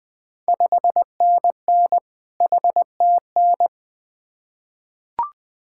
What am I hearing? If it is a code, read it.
Morse code, heard as a keyed tone, about 20 wpm, 700 Hz. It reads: 5NN HTN
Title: Ontario QSO Party Exchange 20wpm